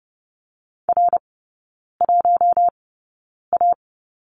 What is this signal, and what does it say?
Morse code, heard as a keyed tone, about 30 wpm, spaced as for 9 wpm, 700 Hz. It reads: R1A
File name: Lesson 17e - Letter u - Three Characters 30wpm